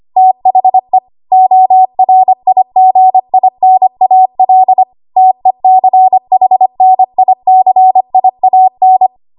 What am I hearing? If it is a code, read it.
Morse code, heard as a keyed tone, about 25 wpm, 750 Hz. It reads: THE ORIGINAL TECHNICIAN
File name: Archive / 25 WPM / 25WPM